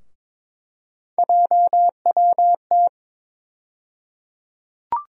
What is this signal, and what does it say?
Morse code, heard as a keyed tone, about 22 wpm, 700 Hz. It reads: JWT
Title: USA Call Sign Postfixes - 3 Letters 22wpm